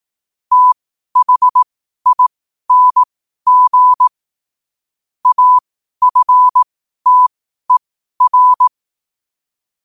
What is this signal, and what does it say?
Morse code, heard as a keyed tone, about 18 wpm, spaced as for 12 wpm, 1000 Hz. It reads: THING AFTER
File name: qotdfiles / 12wpm